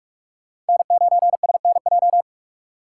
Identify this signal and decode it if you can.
Morse code, heard as a keyed tone, about 43 wpm, 700 Hz. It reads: N9SNJ
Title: USA Call Signs - 5 Characters - 4x Character Spacing - 45wpm